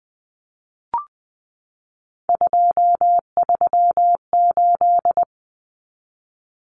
Morse code, heard as a keyed tone, about 20 wpm, 700 Hz. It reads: 238
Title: Numbers - 3-Digit 20wpm